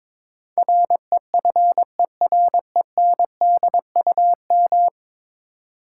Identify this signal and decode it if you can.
Morse code, heard as a keyed tone, about 21 wpm, 700 Hz. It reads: REFERENDUM
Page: morse-code-ninja-practice-22wpm